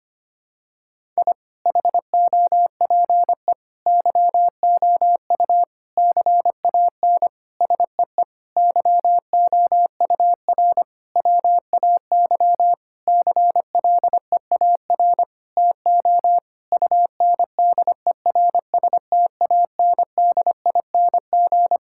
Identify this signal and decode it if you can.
Morse code, heard as a keyed tone, about 25 wpm, 700 Hz. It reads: I HOPE YOU CAN SEE YOUR WAY CLEAR TO UNDERSTANDING
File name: practice / 25wpm / Sentences with 9-Word Phrases - 25wpm